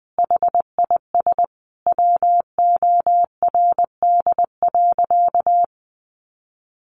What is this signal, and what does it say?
Morse code, heard as a keyed tone, about 20 wpm, 700 Hz. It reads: HIS WORD.
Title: Sentences from Top 1400 Words 20wpm